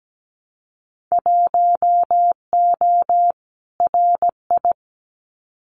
Morse code, heard as a keyed tone, about 17 wpm, 700 Hz. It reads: 1O RI
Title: Winter Field Day Exchanges 17wpm